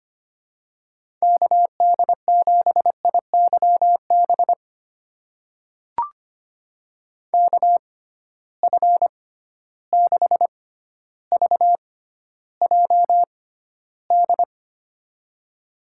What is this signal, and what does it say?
Morse code, heard as a keyed tone, about 25 wpm, spaced as for 13 wpm, 700 Hz. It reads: KD7IYB KF6VJD
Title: USA Call Signs - 6 Characters - 6x Character Spacing - 25wpm